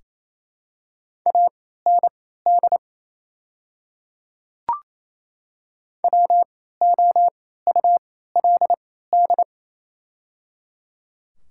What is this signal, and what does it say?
Morse code, heard as a keyed tone, about 28 wpm, spaced as for 16 wpm, 700 Hz. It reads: AND WOULD
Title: 3x Character Spacing - Top 300 Words - 28wpm